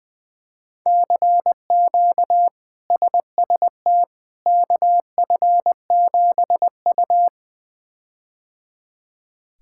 Morse code, heard as a keyed tone, about 20 wpm, 700 Hz. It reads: CQ SST KF7U
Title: K1USN SST QSO Practice - 20wpm